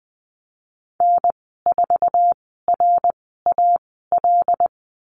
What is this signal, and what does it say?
Morse code, heard as a keyed tone, about 20 wpm, spaced as for 14 wpm, 700 Hz. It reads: N4RAL
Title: 076 Lesson 13g - Letter L - Call Signs 20wpm